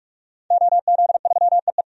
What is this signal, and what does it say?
Morse code, heard as a keyed tone, about 46 wpm, 700 Hz. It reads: OZ3EE
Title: DXCC-100 Call Signs - 5 Characters - 45wpm